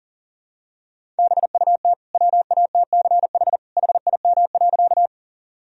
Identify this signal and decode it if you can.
Morse code, heard as a keyed tone, about 40 wpm, 700 Hz. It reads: BUT WATCH HIM.
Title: Sentences from Top 1000 Words - Review Part 1 - 40wpm